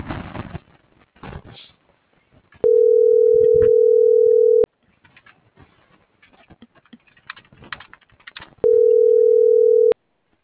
Ringback tone (North America), 440 + 480 Hz.